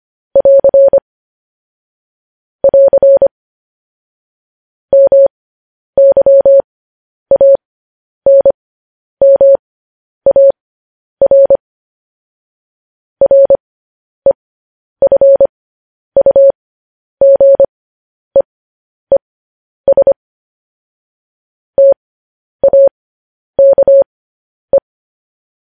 Morse code, heard as a keyed tone, about 25 wpm, spaced as for 10 wpm, 550 Hz. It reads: <AR> <AR> MYANMAR REFUGEES TAKE
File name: podcast / News headlines-1x-25x10